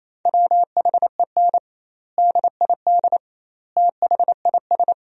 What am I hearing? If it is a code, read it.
Morse code, heard as a keyed tone, about 28 wpm, 700 Hz. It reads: WHEN DID THIS
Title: 2x Word-Spacing - Sentences from Top 400 Words - 28wpm